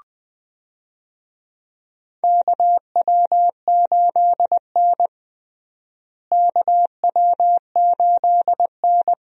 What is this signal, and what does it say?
Morse code, heard as a keyed tone, about 20 wpm, 700 Hz. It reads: KW8N KW8N
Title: USA Call Signs - 4 Characters - Repeated 2x - 20wpm